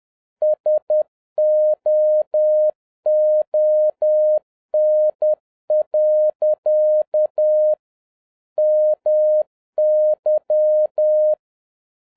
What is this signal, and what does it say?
Morse code, heard as a keyed tone, about 10 wpm, 610 Hz. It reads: SOON. MY